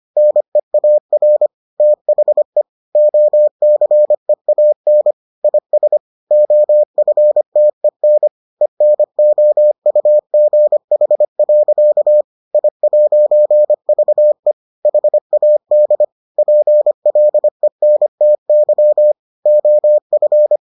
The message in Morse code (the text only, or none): NEAR THE OCEAN IS OFTEN ENOUGH. I'VE HAD PLENTY OF